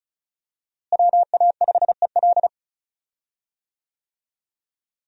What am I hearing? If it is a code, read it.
Morse code, heard as a keyed tone, about 34 wpm, 700 Hz. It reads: WA5EL
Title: USA Call Signs - 5 Characters - Repeated 3x - 35wpm